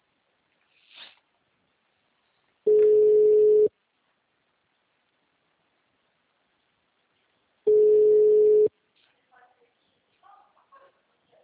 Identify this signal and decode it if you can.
Ringback tone (Europe, 425 Hz), with speech.